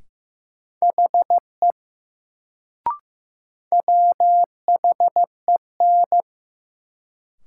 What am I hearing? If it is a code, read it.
Morse code, heard as a keyed tone, about 15 wpm, 700 Hz. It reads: HE WHEN
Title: Sentences from Top 100 Words - New Word Drill - 15wpm